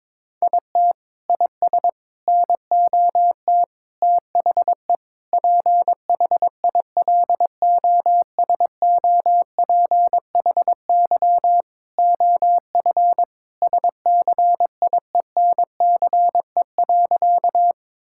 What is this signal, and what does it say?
Morse code, heard as a keyed tone, about 22 wpm, 700 Hz. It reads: IT IS NOT THE PHILOSOPHY OF SCIENCE.
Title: Sentences from Top 1000 Words - Review Part 6 - 22wpm